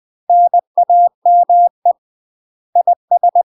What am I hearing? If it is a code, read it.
Morse code, heard as a keyed tone, about 20 wpm, 700 Hz. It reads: NAME IS
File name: UK MASTER QSOs TRAINING / 20WPM / UK MASTER QSOs 0007